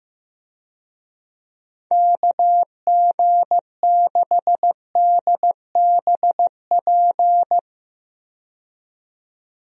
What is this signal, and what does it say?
Morse code, heard as a keyed tone, about 15 wpm, 700 Hz. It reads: KG6DBP
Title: USA Call Signs - 6 Characters - 15wpm